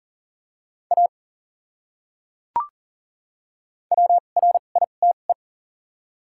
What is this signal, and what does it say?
Morse code, heard as a keyed tone, about 40 wpm, spaced as for 29 wpm, 700 Hz. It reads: A WRITE
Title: 2x Character Spacing - Top 300 Words - 40wpm